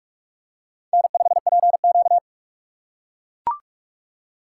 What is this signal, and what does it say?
Morse code, heard as a keyed tone, about 43 wpm, 700 Hz. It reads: N5PQ
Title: USA Call Signs - 2x Character Spacing - 45wpm